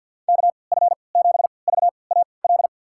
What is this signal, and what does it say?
Morse code, heard as a keyed tone, about 47 wpm, spaced as for 28 wpm, 690 Hz. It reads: KF7VAL
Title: USA Call Signs - 6 Characters - 3x Character Spacing - 50wpm